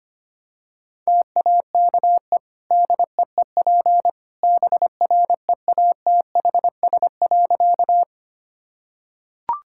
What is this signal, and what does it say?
Morse code, heard as a keyed tone, about 25 wpm, 700 Hz. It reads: TAKE DEEP BREATHS.
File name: Easy Sentences with Hints - 25wpm